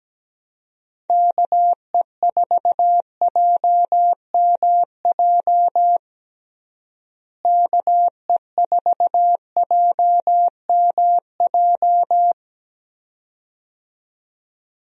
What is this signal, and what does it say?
Morse code, heard as a keyed tone, about 17 wpm, 700 Hz. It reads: KE4JMJ KE4JMJ
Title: USA Call Signs - 6 Characters - Repeated 2x - 17wpm